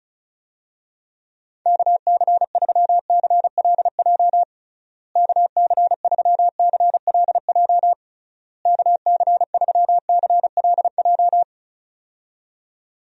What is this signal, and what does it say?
Morse code, heard as a keyed tone, about 34 wpm, 700 Hz. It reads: KC3CLJ KC3CLJ KC3CLJ
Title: USA Call Signs - 6 Characters - Repeated 3x - 35wpm